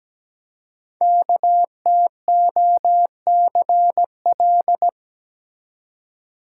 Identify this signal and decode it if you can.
Morse code, heard as a keyed tone, about 17 wpm, 700 Hz. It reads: KTOCL
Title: Code Groups - 5 Letters - 17wpm